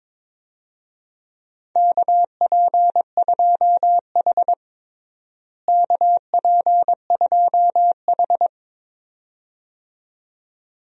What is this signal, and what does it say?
Morse code, heard as a keyed tone, about 22 wpm, 700 Hz. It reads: KP2H KP2H
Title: DXCC-100 Call Signs - 4 Characters - Repeated 2x - 22wpm